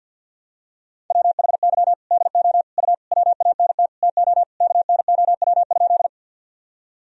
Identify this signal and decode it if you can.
Morse code, heard as a keyed tone, about 51 wpm, 700 Hz. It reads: WHY DO U WANT TO KNOW?
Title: Sentences from Top 200 Words and Abbreviations 50wpm